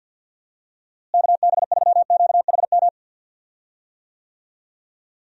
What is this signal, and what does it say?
Morse code, heard as a keyed tone, about 50 wpm, 700 Hz. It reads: KB3QHM
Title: USA Call Signs - 6 Characters - Repeated 3x - 50wpm